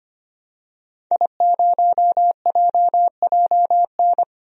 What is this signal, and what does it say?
Morse code, heard as a keyed tone, about 25 wpm, 700 Hz. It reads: I0JJN
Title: DXCC-100 Call Signs - 5 Characters - 25wpm